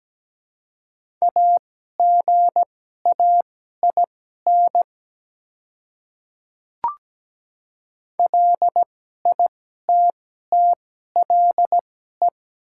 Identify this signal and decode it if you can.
Morse code, heard as a keyed tone, about 17 wpm, spaced as for 12 wpm, 700 Hz. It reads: AGAIN LITTLE
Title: 2x Character Spacing - Top 200 Words - 17wpm